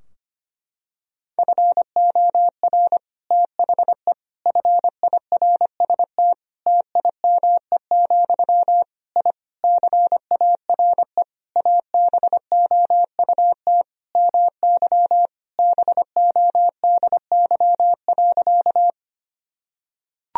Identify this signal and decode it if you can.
Morse code, heard as a keyed tone, about 25 wpm, 700 Hz. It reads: FOR THE FIRST TIME, I CARE ABOUT MY BODY.
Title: Sentences from Top 1000 Words - Review Part 2 - 25wpm